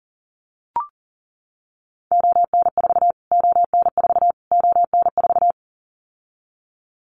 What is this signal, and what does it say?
Morse code, heard as a keyed tone, about 40 wpm, 700 Hz. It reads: ON4 ON4 ON4
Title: Callsign Prefixes for Top 25 DXCC - Repeated 3x - 40wpm